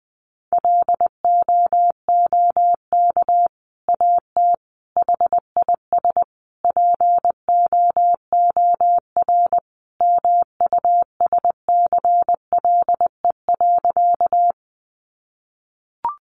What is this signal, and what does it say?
Morse code, heard as a keyed tone, about 20 wpm, 700 Hz. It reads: LOOK AT HIS POOR MUSCLE.
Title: Sentences from Top 1300 Words 20wpm